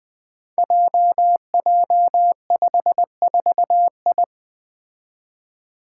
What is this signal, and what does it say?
Morse code, heard as a keyed tone, about 20 wpm, 700 Hz. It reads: JJ54I
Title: Five Challenging Letter-Number Combinations 20wpm